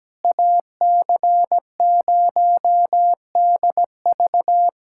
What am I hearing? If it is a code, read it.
Morse code, heard as a keyed tone, about 17 wpm, 700 Hz. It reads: AC0DV
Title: USA Callsign DE Callsign 17wpm